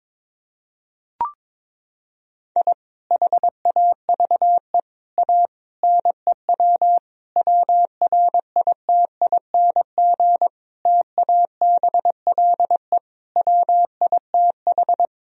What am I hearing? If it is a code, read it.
Morse code, heard as a keyed tone, about 22 wpm, 690 Hz. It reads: I HAVE A NEW WRITING TABLE WITH
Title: Sentences from Top 1000 Words - Review Part 3 - 22wpm